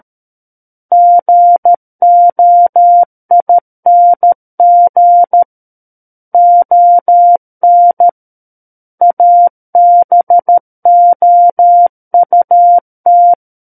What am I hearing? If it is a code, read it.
Morse code, heard as a keyed tone, about 13 wpm, 700 Hz. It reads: GOING ON ABOUT
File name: Book One / 04-Chapter 4 - 13 WPM